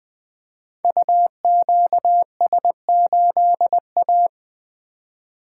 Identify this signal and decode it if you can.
Morse code, heard as a keyed tone, about 20 wpm, 700 Hz. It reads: UQS8A
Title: Code Groups - 5 Letter:Numbers - 20wpm